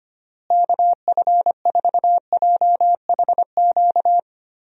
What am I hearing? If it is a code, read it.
Morse code, heard as a keyed tone, about 25 wpm, 700 Hz. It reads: KF4JHQ
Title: USA Call Signs - 6 Characters - Repeated 3x - 25wpm